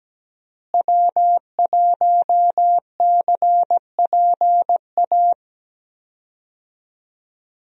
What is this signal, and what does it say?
Morse code, heard as a keyed tone, about 17 wpm, 700 Hz. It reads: W1CPA
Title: USA Call Signs - 5 Characters - Repeated 2x - 17wpm